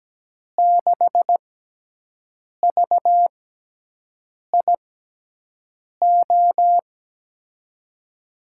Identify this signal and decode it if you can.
Morse code, heard as a keyed tone, about 17 wpm, spaced as for 6 wpm, 700 Hz. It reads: 6VIO
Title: USA Call Signs - 6 Characters - 6x Character Spacing - 17wpm